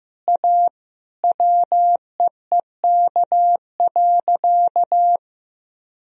Morse code, heard as a keyed tone, about 16 wpm, 700 Hz. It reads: A WEEK.
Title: Sentences from Top 1000 Words - Review Part 2 - 15wpm